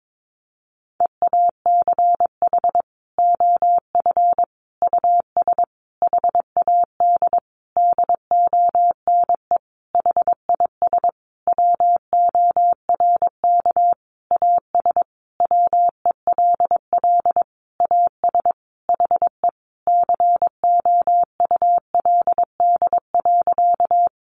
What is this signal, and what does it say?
Morse code, heard as a keyed tone, about 22 wpm, 700 Hz. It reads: EACH OF US HAD DONE HIS WORK AS WELL AS HE COULD.
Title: Sentences from Top 500 Words 22wpm